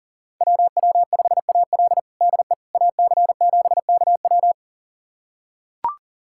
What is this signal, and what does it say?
Morse code, heard as a keyed tone, about 40 wpm, 700 Hz. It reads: WW5AL DE AC7KW
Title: USA Callsign DE Callsign - 5 Characters - 40wpm